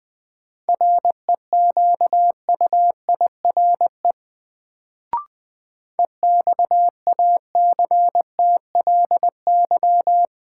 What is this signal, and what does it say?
Morse code, heard as a keyed tone, about 21 wpm, 700 Hz. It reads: REQUIRE EXACTLY
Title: Sentences from Top 900 Words - New Word Drill - 20wpm